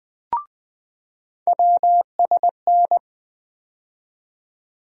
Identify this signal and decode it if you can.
Morse code, heard as a keyed tone, about 20 wpm, 700 Hz. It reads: WSN
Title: USA Call Sign Postfixes - 3 Letters 20wpm